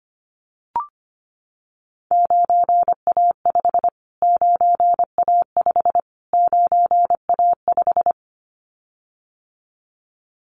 Morse code, heard as a keyed tone, about 25 wpm, 700 Hz. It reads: 9A5 9A5 9A5
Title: Callsign Prefixes for Top 25 DXCC - Repeated 3x - 25wpm